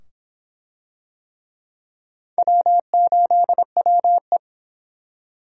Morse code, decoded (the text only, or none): W8WE